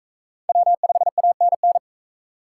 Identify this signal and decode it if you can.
Morse code, heard as a keyed tone, about 40 wpm, 700 Hz. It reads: W5ANN